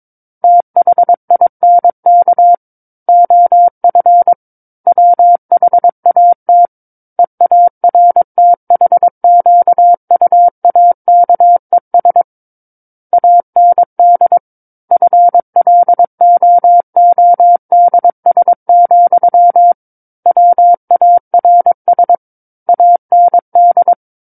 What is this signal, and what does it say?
Morse code, heard as a keyed tone, about 22 wpm, 700 Hz. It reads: THINK OF WHAT EARTHQUAKES AND FLOODS, WARS AND
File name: Book One / 13-Chapter 13 - 22 WPM